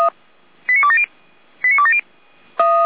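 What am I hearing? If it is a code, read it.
Five-tone selective call (ZVEI), heard as quick runs of five tones, tones of about 70 ms. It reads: 89270 89270
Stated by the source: ZVEI five-tone selective call received by SDR.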